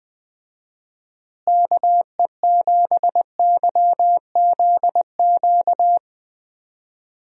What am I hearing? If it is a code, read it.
Morse code, heard as a keyed tone, about 20 wpm, 700 Hz. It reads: KE7YZQ
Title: USA Call Signs - 6 Characters - Repeated 2x - 20wpm